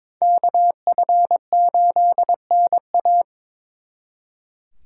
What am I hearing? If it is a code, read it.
Morse code, heard as a keyed tone, about 26 wpm, 700 Hz. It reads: KF8NA
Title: Speed Racing - USA 2x2 Call Signs 22wpm